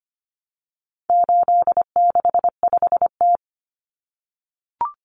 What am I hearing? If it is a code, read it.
Morse code, heard as a keyed tone, about 25 wpm, 700 Hz. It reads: 865T
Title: Common Cut Numbers - 4-Digit 25wpm